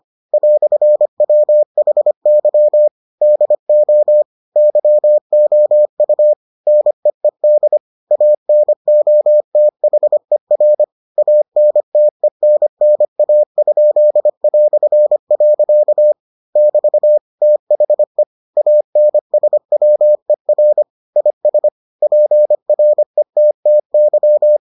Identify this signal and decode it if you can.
Morse code, heard as a keyed tone, about 25 wpm, 600 Hz. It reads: "WHY DO YOU NEED ANOTHER ANTENNA?". <BT> THE ANSWER IS PRETTY